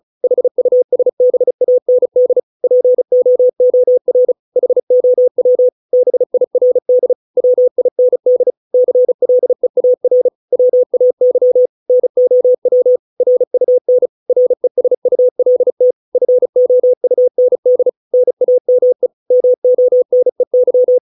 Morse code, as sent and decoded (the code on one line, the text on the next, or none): .... ..- ... -... .- -. -.. / .--. --- --- .-. / .... --- .-- / -... .. .-. -.. / .-- .. -. -.. / -.-. .-.. .- .-. / .-- .- -.-- / -. --- .-- / .-. ..- -. / .-. ... ..- .-.. - / ..-. --- ..- -. -.. / -. .- -- . / -- --- -. -.--
HUSBAND POOR HOW BIRD WIND CLEAR WAY NOW RUN RESULT FOUND NAME MONEY